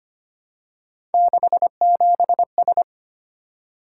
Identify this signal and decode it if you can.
Morse code, heard as a keyed tone, about 25 wpm, 700 Hz. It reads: 67S
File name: Three Letter-Number Combinations - 25wpm